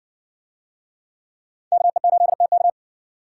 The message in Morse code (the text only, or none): KE8TK